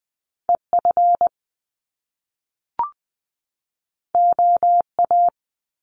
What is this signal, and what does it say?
Morse code, heard as a keyed tone, about 20 wpm, 700 Hz. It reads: EF OA
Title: Top Two Letter Combinations in Words 20wpm